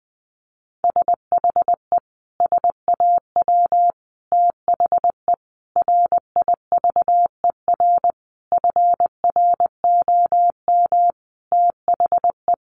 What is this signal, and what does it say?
Morse code, heard as a keyed tone, about 20 wpm, 700 Hz. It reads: SHE SAW THE RIVER FROM THE
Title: Sentences from Top 1200 Words 20wpm